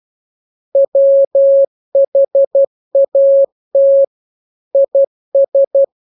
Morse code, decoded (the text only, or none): WHAT IS